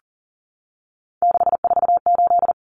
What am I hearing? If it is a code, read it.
Morse code, heard as a keyed tone, about 40 wpm, 700 Hz. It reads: 648
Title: Numbers - 3-Digit 40wpm